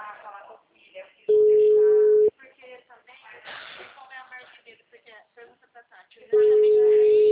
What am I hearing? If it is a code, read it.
Ringback tone (Europe, 425 Hz), with speech.